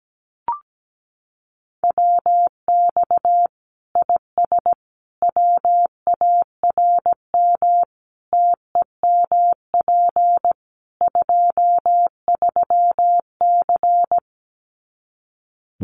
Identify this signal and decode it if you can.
Morse code, heard as a keyed tone, about 17 wpm, 700 Hz. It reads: WX IS WARM TEMP 23C
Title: Common QSO Phrases 17wpm